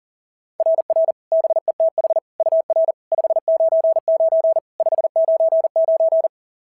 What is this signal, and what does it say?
Morse code, heard as a keyed tone, about 39 wpm, 690 Hz. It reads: RR BETH UR 599 599
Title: SOTA QSO Practice 40wpm